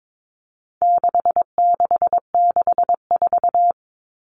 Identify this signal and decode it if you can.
Morse code, heard as a keyed tone, about 22 wpm, 700 Hz. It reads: 6664
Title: Numbers - 4-Digit 22wpm